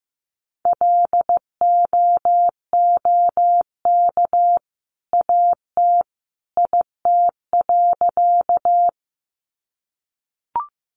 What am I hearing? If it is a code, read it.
Morse code, heard as a keyed tone, about 15 wpm, 700 Hz. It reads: LOOK AT IT.
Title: Sentences from Top 100 Words 15wpm